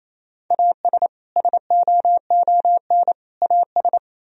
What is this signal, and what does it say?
Morse code, heard as a keyed tone, about 34 wpm, 700 Hz. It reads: AS SOON AS
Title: Speed-Racing - Sets of 3 Words 28wpm